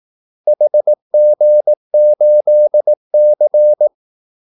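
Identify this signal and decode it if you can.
Morse code, heard as a keyed tone, about 18 wpm, 600 Hz. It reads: HG8C
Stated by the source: Morse call-sign practice at 18 wpm.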